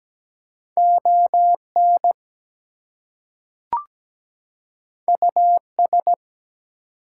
Morse code, heard as a keyed tone, about 17 wpm, 700 Hz. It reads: ON US
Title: Two Letter Words - 17wpm